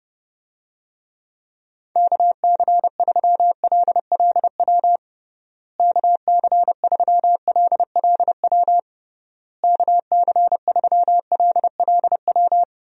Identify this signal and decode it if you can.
Morse code, heard as a keyed tone, about 30 wpm, 700 Hz. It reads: KC3LLW KC3LLW KC3LLW
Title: USA Call Signs - 6 Characters - Repeated 3x - 30wpm